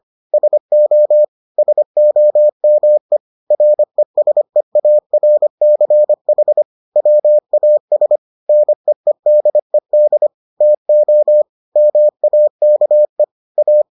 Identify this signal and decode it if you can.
Morse code, heard as a keyed tone, about 25 wpm, 600 Hz. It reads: SO SOME RESEARCH WAS NEEDED TO MAKE A